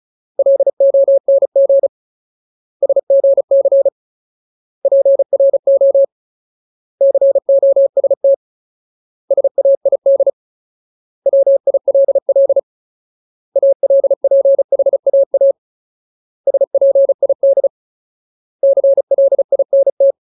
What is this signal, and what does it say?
Morse code, heard as a keyed tone, about 35 wpm, 550 Hz. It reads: LONG SGC PRO COST SAID WILL ALPHAA SPID CLINT
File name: Ultimate Word List / 35WPM Ultimate